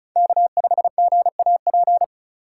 Morse code, heard as a keyed tone, about 36 wpm, 690 Hz. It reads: K5GAP
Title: USA Call Signs - 5 Characters - 3x Character Spacing - 35wpm